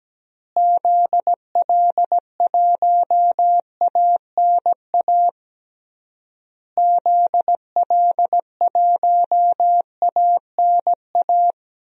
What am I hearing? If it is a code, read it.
Morse code, heard as a keyed tone, about 17 wpm, 700 Hz. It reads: ZL1ANA ZL1ANA
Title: DXCC-100 Call Signs - 6 Characters - Repeated 3x - 17wpm